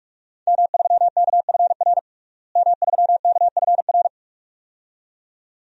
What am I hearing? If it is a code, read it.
Morse code, heard as a keyed tone, about 46 wpm, 700 Hz. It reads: M3KFR M3KFR
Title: DXCC-100 Call Signs - 5 Characters - Repeated 2x - 45wpm